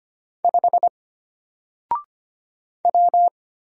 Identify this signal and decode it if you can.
Morse code, heard as a keyed tone, about 25 wpm, 700 Hz. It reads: 5 W
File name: Single-Letter-Number-25wpm